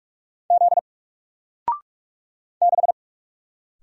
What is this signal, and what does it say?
Morse code, heard as a keyed tone, about 44 wpm, 700 Hz. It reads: Z 6
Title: Single Letter-Number Alt 45wpm